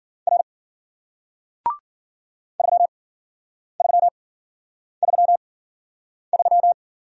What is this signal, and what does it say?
Morse code, heard as a keyed tone, about 53 wpm, spaced as for 9 wpm, 700 Hz. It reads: R 3333